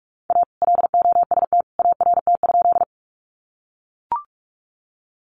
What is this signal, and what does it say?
Morse code, heard as a keyed tone, about 44 wpm, 700 Hz. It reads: A LOST ART?
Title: Sentences from Top 1400 Words 45wpm